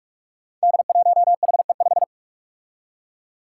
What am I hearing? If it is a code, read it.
Morse code, heard as a keyed tone, about 46 wpm, 700 Hz. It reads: D1HE5
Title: Five Challenging Letter-Number Combinations 45wpm